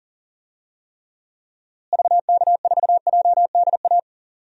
Morse code, heard as a keyed tone, about 40 wpm, 700 Hz. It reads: VK4JDA
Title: DXCC-100 Call Signs - 6 Characters - Repeated 3x - 40wpm